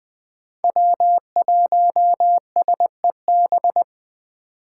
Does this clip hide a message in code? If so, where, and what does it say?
Yes: Morse code from 0.6 to 3.8 s, heard as a keyed tone, about 21 wpm, 700 Hz. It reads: W1SEB